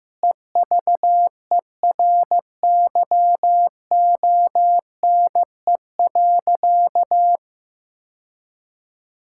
Morse code, heard as a keyed tone, about 15 wpm, 700 Hz. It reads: EVERYONE.
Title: Sentences from Top 1000 Words - Review Part 5 - 15wpm